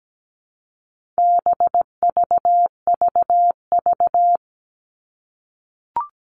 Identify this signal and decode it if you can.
Morse code, heard as a keyed tone, about 17 wpm, 700 Hz. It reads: BVVV
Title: All Cut Numbers - 4-Digit 17wpm